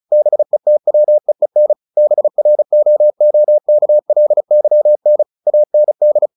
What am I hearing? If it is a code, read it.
Morse code, heard as a keyed tone, about 34 wpm, 600 Hz. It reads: BETWEEN BROOKLYN AND
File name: the old man and the sea / the old man and the sea 1 35wpm